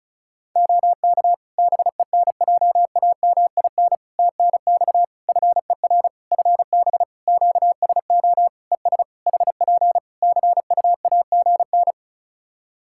Morse code, heard as a keyed tone, about 36 wpm, 700 Hz. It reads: OK BENJAMIN TNX FER FB QSO ES HP CUAGN
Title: Practice QSO - Exchange 4 - Ending - 35wpm